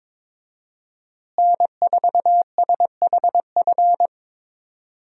Morse code, heard as a keyed tone, about 22 wpm, 700 Hz. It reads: N4SHF
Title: USA Call Signs - 5 Characters - Repeated 2x - 22wpm